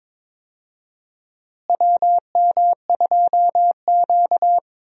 Morse code, heard as a keyed tone, about 22 wpm, 700 Hz. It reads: WM2Q
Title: USA Call Signs - 4 Characters - Repeated 2x - 22wpm